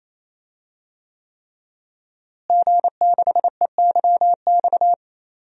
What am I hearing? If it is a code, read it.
Morse code, heard as a keyed tone, about 28 wpm, 700 Hz. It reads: G6EYX
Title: DXCC-100 Call Signs - 5 Characters - Repeated 3x - 28wpm